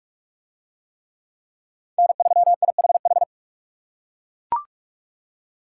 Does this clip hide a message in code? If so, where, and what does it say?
Yes: Morse code from 2.0 to 3.2 s, heard as a keyed tone, about 39 wpm, 700 Hz. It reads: N3IHH